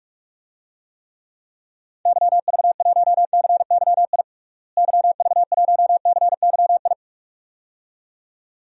Morse code, heard as a keyed tone, about 41 wpm, 700 Hz. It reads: YV1CYI YV1CYI
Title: DXCC-100 Call Signs - 6 Characters - Repeated 2x - 45wpm